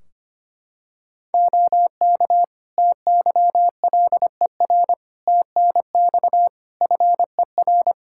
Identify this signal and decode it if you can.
Morse code, heard as a keyed tone, about 25 wpm, 700 Hz. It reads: OK TYLER TNX FER
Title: Practice QSO - Exchange 4 - Ending - 25wpm